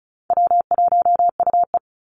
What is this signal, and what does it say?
Morse code, heard as a keyed tone, about 34 wpm, 700 Hz. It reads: W1UE
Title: Call Signs - 35wpm